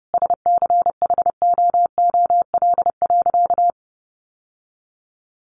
Morse code, heard as a keyed tone, about 30 wpm, 700 Hz. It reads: SCHOOL.
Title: Sentences from Top 1100 Words 30wpm